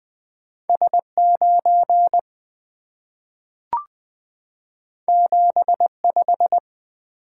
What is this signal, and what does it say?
Morse code, heard as a keyed tone, about 20 wpm, 700 Hz. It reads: S9 75
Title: Two Letter-Number Combinations - 20wpm